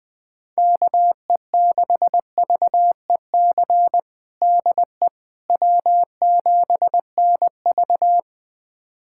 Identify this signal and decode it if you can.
Morse code, heard as a keyed tone, about 21 wpm, 700 Hz. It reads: KE6VEC DE W7NV
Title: USA Callsign DE Callsign 20wpm